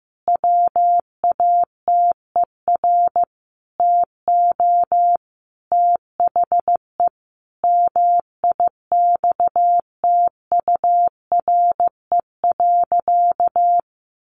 Morse code, heard as a keyed tone, about 15 wpm, 700 Hz. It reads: WATER TO THE MIXTURE.
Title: Sentences from Top 1500 Words 15wpm